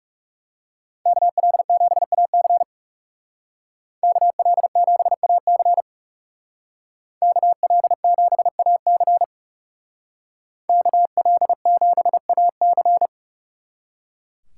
Morse code, heard as a keyed tone, about 36 wpm, 700 Hz. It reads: KL7AC KL7AC KL7AC KL7AC
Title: Speed Racing - USA 2x2 Call Signs 30wpm